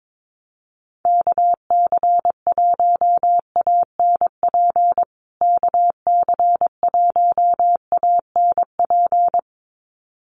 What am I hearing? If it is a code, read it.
Morse code, heard as a keyed tone, about 22 wpm, 700 Hz. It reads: KC1ANP KC1ANP
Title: USA Call Signs Repeated 2x 22wpm